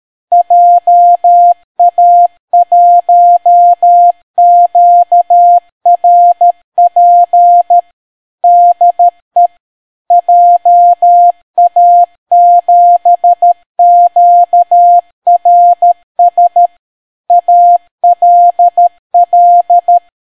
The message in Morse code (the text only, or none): JA1QRP DE JA7QRS ALL